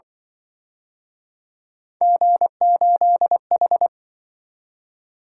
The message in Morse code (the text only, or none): G8H